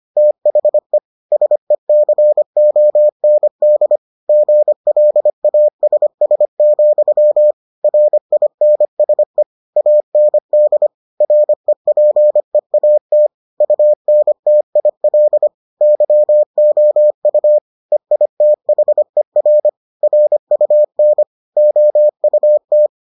Morse code, heard as a keyed tone, about 25 wpm, 600 Hz. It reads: THE SECOND GLASS, RINSE AND REPEAT UNTIL YOU EITHER RUN OUT